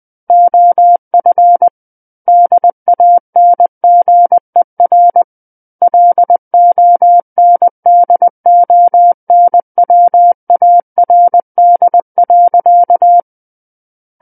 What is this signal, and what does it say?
Morse code, heard as a keyed tone, about 20 wpm, 700 Hz. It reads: OF DANGER LONDONWARD.
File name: 20 WPM